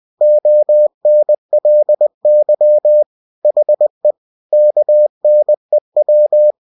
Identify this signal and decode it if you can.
Morse code, heard as a keyed tone, about 20 wpm, 600 Hz. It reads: ONLY HE KNEW
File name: cw-books / winnie-the-pooh / winnie-20wpm-page-0036